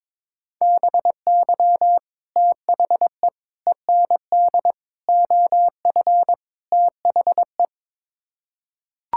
Morse code, heard as a keyed tone, about 21 wpm, 700 Hz. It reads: BY THE END OF THE
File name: practice / 22wpm / Build A Phrase - 5-Word Phrases - 22wpm